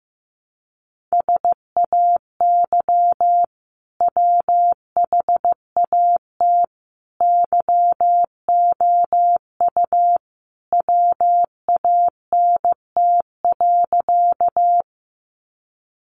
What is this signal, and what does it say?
Morse code, heard as a keyed tone, about 15 wpm, 700 Hz. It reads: SAY WHAT YOU WANT.
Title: Sentences from Top 200 Words 15wpm